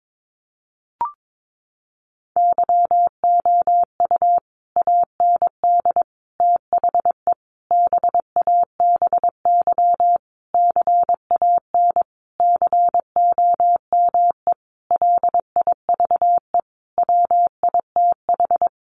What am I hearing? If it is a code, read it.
Morse code, heard as a keyed tone, about 22 wpm, 700 Hz. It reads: YOU AND THE BABY CAN COME LIVE WITH
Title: Sentences from Top 1300 Words 22wpm